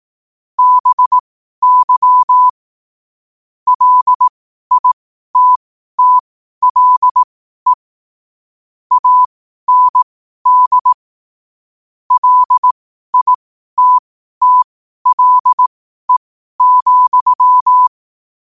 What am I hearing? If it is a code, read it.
Morse code, heard as a keyed tone, about 18 wpm, spaced as for 12 wpm, 1000 Hz. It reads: BY LITTLE AND LITTLE,
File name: qotdfiles / 12wpm